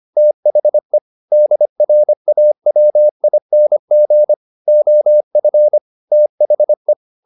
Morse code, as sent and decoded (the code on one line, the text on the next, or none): - .... / -.. .-. .- .-- .. -. --. / --- ..-. / - .... .
THE DRAWING OF THE